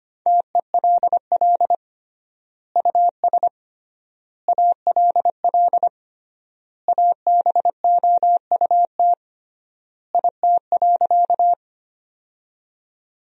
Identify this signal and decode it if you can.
Morse code, heard as a keyed tone, about 25 wpm, 700 Hz. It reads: TELL US ALL ABOUT IT.
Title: 3x Word-Spacing - Sentences from Top 200 Words - 25wpm